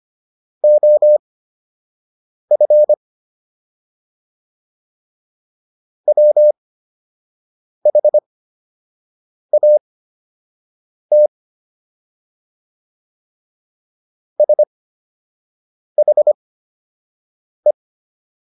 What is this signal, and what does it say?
Morse code, heard as a keyed tone, about 25 wpm, spaced as for 6 wpm, 600 Hz. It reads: OF WHAT SHE